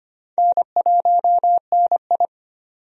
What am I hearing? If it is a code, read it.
Morse code, heard as a keyed tone, about 25 wpm, 700 Hz. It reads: N1NI
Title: USA Call Signs - 4 Characters - 25wpm